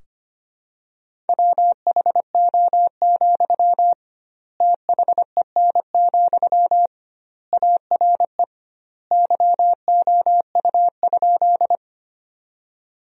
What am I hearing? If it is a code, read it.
Morse code, heard as a keyed tone, about 25 wpm, 700 Hz. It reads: WHO, THEN, ARE YOU?
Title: 2x Word-Spacing - Sentences from Top 100 Words - 25wpm